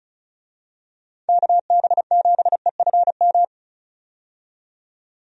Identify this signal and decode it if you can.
Morse code, heard as a keyed tone, about 34 wpm, 700 Hz. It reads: KB7EFM
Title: USA Call Signs - 6 Characters - 35wpm